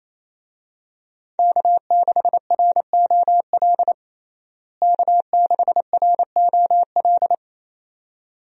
Morse code, heard as a keyed tone, about 28 wpm, 700 Hz. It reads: K6ROL K6ROL
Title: USA Call Signs - 5 Characters - Repeated 2x - 28wpm